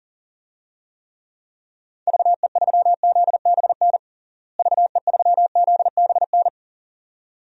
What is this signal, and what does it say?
Morse code, heard as a keyed tone, about 40 wpm, 700 Hz. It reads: VE3ZBN VE3ZBN
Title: DXCC-100 Call Signs - 6 Characters - Repeated 2x - 40wpm